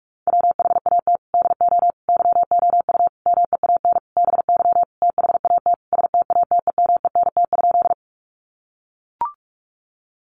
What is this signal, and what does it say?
Morse code, heard as a keyed tone, about 44 wpm, 700 Hz. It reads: WHAT DO YOU MEAN BY THAT STATEMENT?
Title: Sentences from Top 1200 Words 45wpm